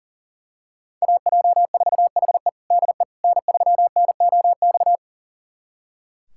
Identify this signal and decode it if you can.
Morse code, heard as a keyed tone, about 37 wpm, 700 Hz. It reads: AJ4HE DE N3NOX
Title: USA Callsign DE Callsign - 5 Characters - 40wpm